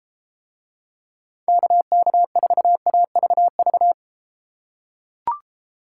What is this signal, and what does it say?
Morse code, heard as a keyed tone, about 33 wpm, 700 Hz. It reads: KK4AVV